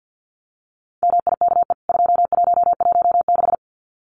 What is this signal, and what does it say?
Morse code, heard as a keyed tone, about 49 wpm, 700 Hz. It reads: MIKE 2116
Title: CWT Exchange 50wpm